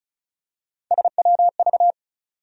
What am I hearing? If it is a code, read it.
Morse code, heard as a keyed tone, about 34 wpm, 700 Hz. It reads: SWV
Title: USA Call Sign Postfixes - 3 Letters 35wpm